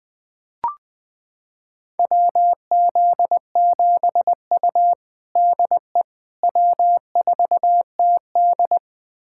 Morse code, heard as a keyed tone, about 21 wpm, 700 Hz. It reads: WZ7U DE W4TD
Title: USA Callsign DE Callsign - 4 Characters - 20wpm